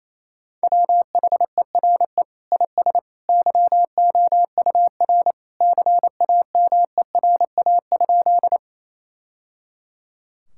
Morse code, heard as a keyed tone, about 28 wpm, 700 Hz. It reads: WHERE IS YOUR CAMERA?
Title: Sentences from Top 1000 Words - Review Part 5 - 28wpm